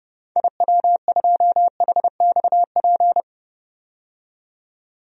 Morse code, heard as a keyed tone, about 32 wpm, 700 Hz. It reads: IW2HXP
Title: DXCC-100 Call Signs - 6 Characters - 30wpm